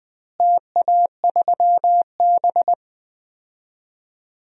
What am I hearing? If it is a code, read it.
Morse code, heard as a keyed tone, about 21 wpm, 700 Hz. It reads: TA3B
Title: DXCC-100 Call Signs - 4 Characters - Repeated 3x - 20wpm